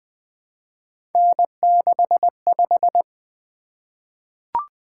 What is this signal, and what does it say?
Morse code, heard as a keyed tone, about 20 wpm, 700 Hz. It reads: N65
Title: Three Letter-Number Combinations - 20wpm